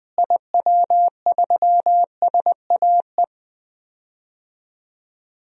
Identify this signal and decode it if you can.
Morse code, heard as a keyed tone, about 20 wpm, 700 Hz. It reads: IW3SAE
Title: DXCC-100 Call Signs - 6 Characters - Repeated 3x - 20wpm